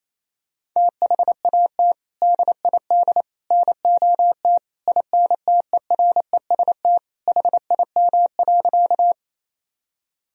Morse code, heard as a keyed tone, about 28 wpm, 700 Hz. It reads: THAT DID NOT INTEREST HIM.
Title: Sentences from Top 1000 Words - Review Part 2 - 28wpm